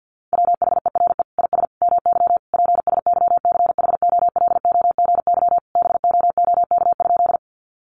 Morse code, heard as a keyed tone, about 50 wpm, 700 Hz. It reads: WHERE IS MY PSYCHOLOGY BOOK?
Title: Sentences from Top 1400 Words 50wpm